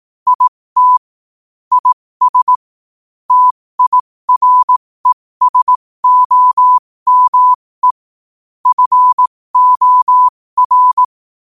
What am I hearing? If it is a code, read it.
Morse code, heard as a keyed tone, about 18 wpm, spaced as for 15 wpm, 1000 Hz. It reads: IT IS TIRESOME FOR